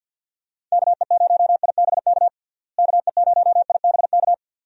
Morse code, heard as a keyed tone, about 51 wpm, 700 Hz. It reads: KE0IBK KE0IBK
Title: USA Call Signs - 6 Characters - Repeated 2x - 50wpm